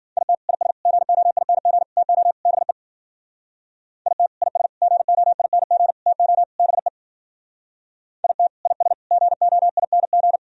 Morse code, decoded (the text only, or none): IT IS GOING TO BE IT IS GOING TO BE IT IS GOING